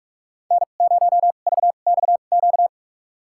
Morse code, heard as a keyed tone, about 47 wpm, spaced as for 33 wpm, 700 Hz. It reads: N0VXQ
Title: USA Call Signs - 5 Characters - 2x Character Spacing - 45wpm